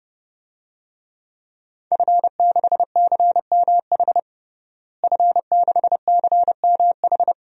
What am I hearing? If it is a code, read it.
Morse code, heard as a keyed tone, about 32 wpm, 700 Hz. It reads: F6CMH F6CMH
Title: DXCC-100 Call Signs - 5 Characters - Repeated 3x - 30wpm